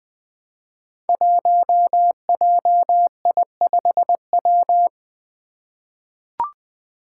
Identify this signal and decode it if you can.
Morse code, heard as a keyed tone, about 20 wpm, 700 Hz. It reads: 1JI5W
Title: Five Challenging Letter-Number Combinations 20wpm